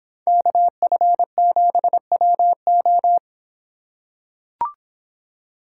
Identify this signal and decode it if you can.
Morse code, heard as a keyed tone, about 25 wpm, 700 Hz. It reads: KF7WO